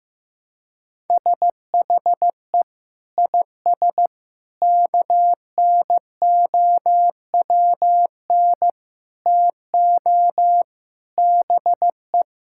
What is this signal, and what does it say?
Morse code, heard as a keyed tone, about 16 wpm, 700 Hz. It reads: SHE IS KNOWN TO BE
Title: Sentences from Top 1000 Words - Review Part 5 - 15wpm